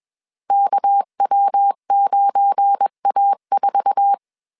Morse code, heard as a keyed tone, about 21 wpm, 800 Hz. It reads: KW9A4